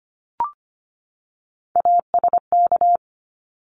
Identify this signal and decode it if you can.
Morse code, heard as a keyed tone, about 25 wpm, 700 Hz. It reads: ASK
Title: ED Suffix - Top 100 Words with Base Words 25wpm